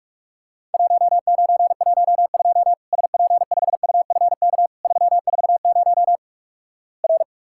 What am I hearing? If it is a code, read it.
Morse code, heard as a keyed tone, about 48 wpm, 700 Hz. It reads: 1912 SP5UFK 340 R
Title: Worked All Europe DX Contest - QTC Exchange - 45wpm